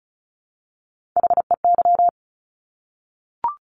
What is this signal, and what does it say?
Morse code, heard as a keyed tone, about 36 wpm, 700 Hz. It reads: HEY
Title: Top Three Letter Combinations in Words 35wpm